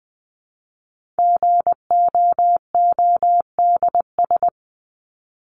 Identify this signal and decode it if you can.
Morse code, heard as a keyed tone, about 20 wpm, 700 Hz. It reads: GOODS
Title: ES:S Suffix - Top 100 Words with Base Words 20wpm